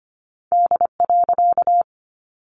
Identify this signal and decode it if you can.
Morse code, heard as a keyed tone, about 25 wpm, 700 Hz. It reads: D.
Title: A Sound Of Thunder 25wpm